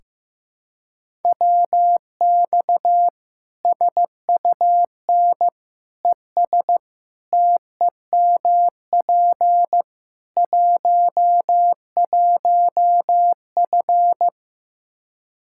Morse code, heard as a keyed tone, about 15 wpm, 700 Hz. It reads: WX SUN ES TEMP 11F
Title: Practice QSO - Exchange 2 - Info - 15wpm